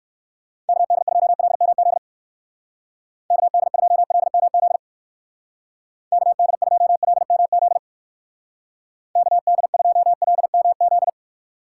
Morse code, heard as a keyed tone, about 59 wpm, 700 Hz. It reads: KD2LMZ KD2LMZ KD2LMZ KD2LMZ